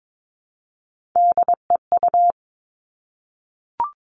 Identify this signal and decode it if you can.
Morse code, heard as a keyed tone, about 22 wpm, 700 Hz. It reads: DEU
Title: Three Challenging Letter-Number Combinations 22wpm